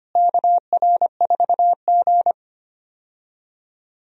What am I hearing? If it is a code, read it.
Morse code, heard as a keyed tone, about 25 wpm, 700 Hz. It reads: KR4G
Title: USA Call Signs - 4 Characters - Repeated 3x - 25wpm